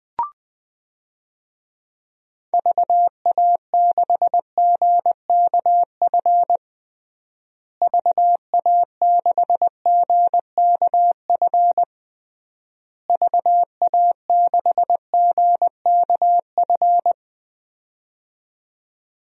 Morse code, heard as a keyed tone, about 20 wpm, 700 Hz. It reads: VA6GKF VA6GKF VA6GKF
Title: DXCC-100 Call Signs - 6 Characters - Repeated 3x - 20wpm